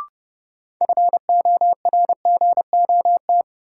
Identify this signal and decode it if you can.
Morse code, heard as a keyed tone, about 30 wpm, 700 Hz. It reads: FORGOT
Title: Sentences from Top 1400 Words - New Word Drill - 30wpm